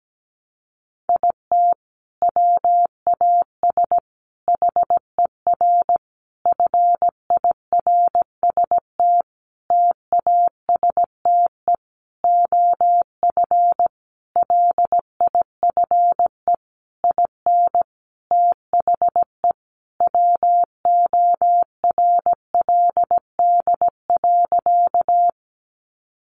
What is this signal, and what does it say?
Morse code, heard as a keyed tone, about 17 wpm, 700 Hz. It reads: IT WAS HER FIRST TASTE OF LIFE IN THE WORLD.
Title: Sentences from Top 1100 Words 17wpm